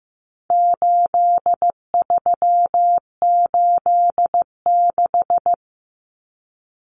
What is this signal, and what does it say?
Morse code, heard as a keyed tone, about 15 wpm, 700 Hz. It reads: 8386
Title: Numbers - 4-Digit 15wpm